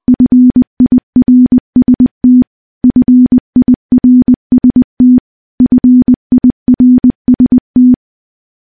Morse code, heard as a keyed tone, about 20 wpm, 250 Hz. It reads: FIRST FIRST FIRST
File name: morse / morse-training-20WPM